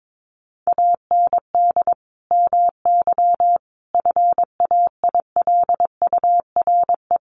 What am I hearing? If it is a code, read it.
Morse code, heard as a keyed tone, about 22 wpm, 700 Hz. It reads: AND MY FAILURE
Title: Sentences from Top 1400 Words 22wpm